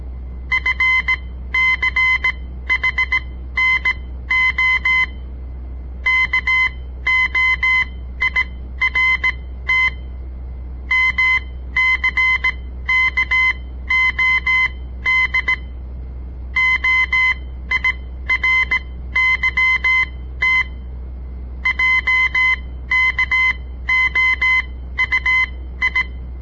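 Morse code, heard as a keyed tone, about 17 wpm, spaced as for 13 wpm, 2050 Hz. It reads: FCHNO KOIRT MCKOD OIRYT JKOUI
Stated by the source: On-air Morse code practice broadcast by a southern Montana amateur radio association; all code, no read back.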